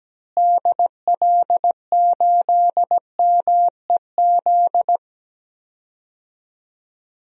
Morse code, heard as a keyed tone, about 17 wpm, 700 Hz. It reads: DL8MEZ
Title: DXCC-100 Call Signs - 6 Characters - Repeated 2x - 17wpm